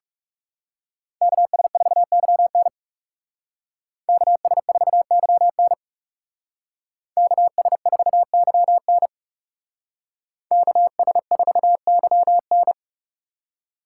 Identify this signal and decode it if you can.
Morse code, heard as a keyed tone, about 39 wpm, 700 Hz. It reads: KS4YN KS4YN KS4YN KS4YN